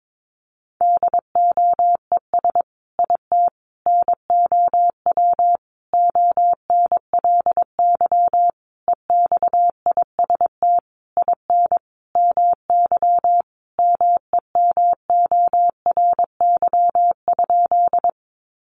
Morse code, heard as a keyed tone, about 22 wpm, 700 Hz. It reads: DOES IT NOW ONLY EXIST IN MY MEMORY?
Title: Sentences from Top 1300 Words 22wpm